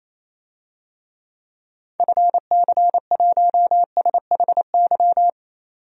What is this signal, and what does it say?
Morse code, heard as a keyed tone, about 28 wpm, 700 Hz. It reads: FC1SHY
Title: DXCC-100 Call Signs - 6 Characters - 28wpm